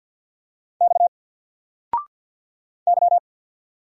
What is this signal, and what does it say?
Morse code, heard as a keyed tone, about 48 wpm, 700 Hz. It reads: X Y
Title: Single Character (Letter:Number:Punctuation:Prosign) Alt 50wpm